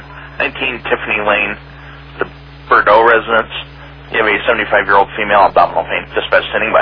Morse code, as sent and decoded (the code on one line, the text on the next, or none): none